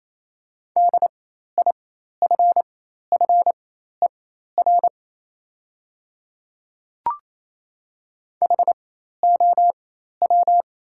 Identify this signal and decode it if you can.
Morse code, heard as a keyed tone, about 27 wpm, spaced as for 13 wpm, 690 Hz. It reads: DIFFER HOW
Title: 4x Character Spacing - Top 200 Words - 28wpm